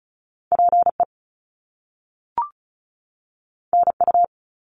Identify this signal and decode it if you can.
Morse code, heard as a keyed tone, about 36 wpm, 700 Hz. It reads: PE NU
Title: Canadian Provinces and Territories 35wpm